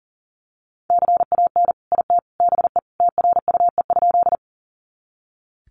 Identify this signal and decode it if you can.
Morse code, heard as a keyed tone, about 40 wpm, 700 Hz. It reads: CAN IT BE TRUE?
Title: Sentences from Top 400 Words 40wpm